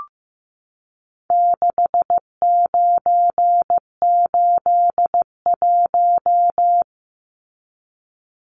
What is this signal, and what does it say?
Morse code, heard as a keyed tone, about 15 wpm, 700 Hz. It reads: 6981
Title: Numbers - 4-Digit 15wpm